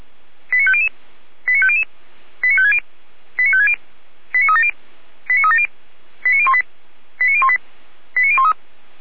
Five-tone selective call (ZVEI), heard as quick runs of five tones, tones of about 70 ms. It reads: 89500 89500 89560 89560 89370 89370 89018 89018 89013